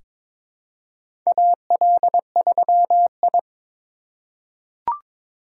Morse code, heard as a keyed tone, about 22 wpm, 700 Hz. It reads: AL3I
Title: USA Call Signs - 4 Characters - 3x Character Spacing - 22wpm